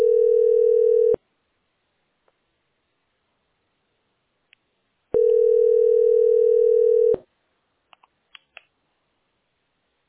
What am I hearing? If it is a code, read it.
Telephone call progress tone (North America), heard as a telephone network's tone, about 440, 480 Hz. Ringback tone.